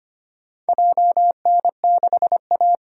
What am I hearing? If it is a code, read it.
Morse code, heard as a keyed tone, about 25 wpm, 700 Hz. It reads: JN6A
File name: practice / 25wpm / Four Letter-Number Combinations - 25wpm